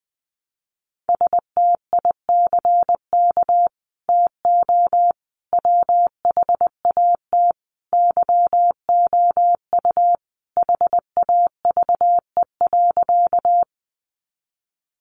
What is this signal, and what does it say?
Morse code, heard as a keyed tone, about 20 wpm, 700 Hz. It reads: STICK TO WHAT YOU HAVE.
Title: Sentences from Top 1200 Words 20wpm